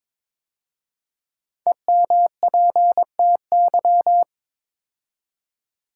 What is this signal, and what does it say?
Morse code, heard as a keyed tone, about 22 wpm, 700 Hz. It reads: EMPTY
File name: 5 Letter Words - First 3 Letters Given - 22wpm